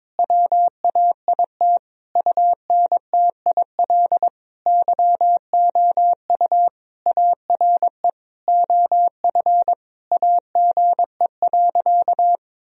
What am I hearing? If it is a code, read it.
Morse code, heard as a keyed tone, about 22 wpm, 700 Hz. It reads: WAIT UNTIL YOU ARE OF AGE.
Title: Sentences from Top 1000 Words - Review Part 3 - 22wpm